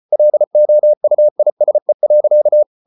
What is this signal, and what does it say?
Morse code, heard as a keyed tone, about 34 wpm, 600 Hz. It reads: LOUISE.
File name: Ferie QRQ 34wpm